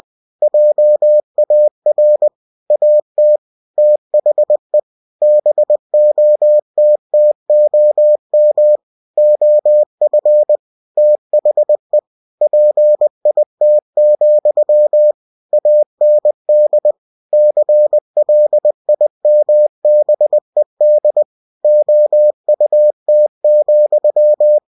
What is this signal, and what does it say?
Morse code, heard as a keyed tone, about 20 wpm, 600 Hz. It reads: JAR AT THE BOTTOM OF THE PIT, AND CLIMBED OUT,
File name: cw-books / winnie-the-pooh / winnie-20wpm-page-0038